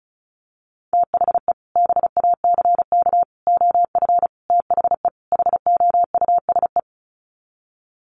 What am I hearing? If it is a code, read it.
Morse code, heard as a keyed tone, about 34 wpm, 700 Hz. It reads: THE BACK OF THE HOUSE
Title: Sets of 5 Words Encore2 35wpm